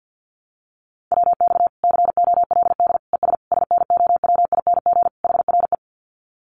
Morse code, heard as a keyed tone, about 50 wpm, 700 Hz. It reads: WX COLD ES SNOWING HRE
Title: Common QSO Phrases 50wpm